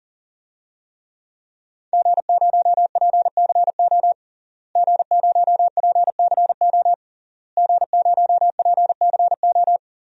Morse code, heard as a keyed tone, about 40 wpm, 700 Hz. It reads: G0PCO G0PCO G0PCO
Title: DXCC-100 Call Signs - 5 Characters - Repeated 3x - 40wpm